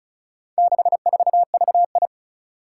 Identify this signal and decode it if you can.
Morse code, heard as a keyed tone, about 36 wpm, 700 Hz. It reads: 64VI
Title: Four Challenging Letter-Number Combinations 35wpm